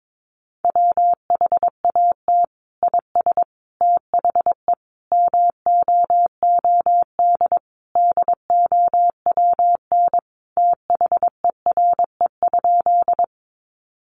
Morse code, heard as a keyed tone, about 22 wpm, 700 Hz. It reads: WHAT IS THE MOOD DOWN THERE?
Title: Sentences from Top 1100 Words 22wpm